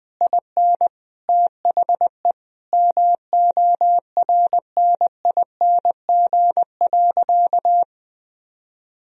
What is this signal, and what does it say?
Morse code, heard as a keyed tone, about 20 wpm, 700 Hz. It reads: IN THE MORNING.
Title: Sentences from Top 1000 Words - Review Part 2 - 20wpm